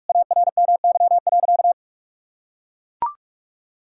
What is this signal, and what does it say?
Morse code, heard as a keyed tone, about 41 wpm, 700 Hz. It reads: ARMY.
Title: Sentences from Top 1000 Words - Review Part 5 - 45wpm